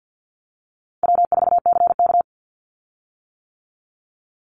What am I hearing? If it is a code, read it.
Morse code, heard as a keyed tone, about 50 wpm, 700 Hz. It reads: W4CK
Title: Call Signs - 50wpm